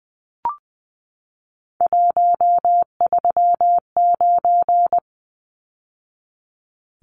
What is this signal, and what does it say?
Morse code, heard as a keyed tone, about 20 wpm, 700 Hz. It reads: 139
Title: Numbers - 3-Digit 20wpm